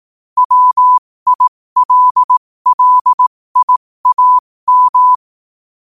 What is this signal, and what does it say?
Morse code, heard as a keyed tone, about 18 wpm, spaced as for 15 wpm, 1000 Hz. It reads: WILLIAM
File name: qotdfiles / 15wpm